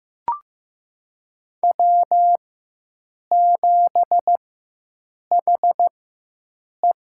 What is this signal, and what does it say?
Morse code, heard as a keyed tone, about 15 wpm, spaced as for 7 wpm, 700 Hz. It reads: W7HE
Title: USA Call Signs - 4 Characters - 4x Character Spacing - 15wpm